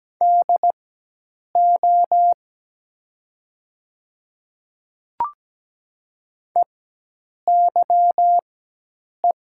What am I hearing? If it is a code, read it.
Morse code, heard as a keyed tone, about 17 wpm, spaced as for 8 wpm, 700 Hz. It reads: DO EYE
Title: 4x Character Spacing - Top 500 Words - 17wpm